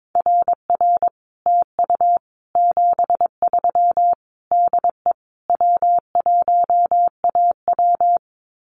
Morse code, heard as a keyed tone, about 22 wpm, 700 Hz. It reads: RR TU 73 DE W1AW
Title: Common QSO Phrases Encore 22wpm